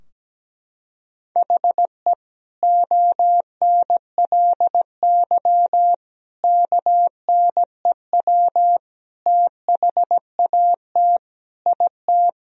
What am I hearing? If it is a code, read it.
Morse code, heard as a keyed tone, about 17 wpm, 700 Hz. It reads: HE ONLY KNEW THAT IT
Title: Sentences from Top 1000 Words - Review Part 2 - 17wpm